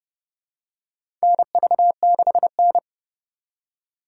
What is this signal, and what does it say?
Morse code, heard as a keyed tone, about 32 wpm, 700 Hz. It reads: NV6N
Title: USA Call Signs - 4 Characters - 3x Character Spacing - 30wpm